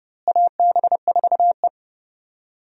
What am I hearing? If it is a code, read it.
Morse code, heard as a keyed tone, about 30 wpm, 700 Hz. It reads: AB4E
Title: USA Call Signs - 4 Characters - Repeated 3x - 30wpm